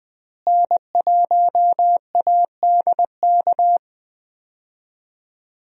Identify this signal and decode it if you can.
Morse code, heard as a keyed tone, about 20 wpm, 700 Hz. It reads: N1ADK
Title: USA Call Signs - 5 Characters - Repeated 3x - 20wpm